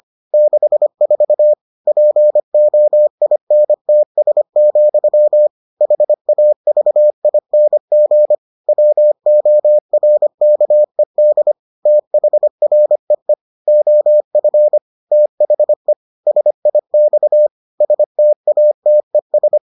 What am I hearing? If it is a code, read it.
Morse code, heard as a keyed tone, about 25 wpm, 600 Hz. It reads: 64 POINTS, HAVING WORKED THREE OF THE SIX STATES